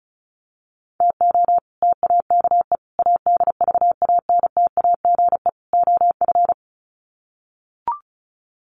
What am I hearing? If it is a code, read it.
Morse code, heard as a keyed tone, about 34 wpm, 700 Hz. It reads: TO TAKE ADVANTAGE OF
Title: Sets of 4 Words 35wpm